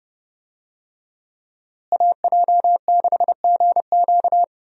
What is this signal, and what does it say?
Morse code, heard as a keyed tone, about 30 wpm, 700 Hz. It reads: AJ6GQ
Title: USA Call Signs - 5 Characters - Repeated 3x - 30wpm